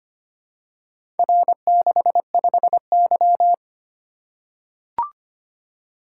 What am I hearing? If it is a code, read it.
Morse code, heard as a keyed tone, about 25 wpm, 690 Hz. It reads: R65Y
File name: practice / 25wpm / Four Letter-Number Combinations - 25wpm